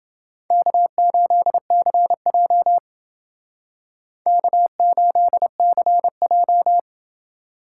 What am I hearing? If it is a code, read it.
Morse code, heard as a keyed tone, about 28 wpm, 700 Hz. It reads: K8CJ K8CJ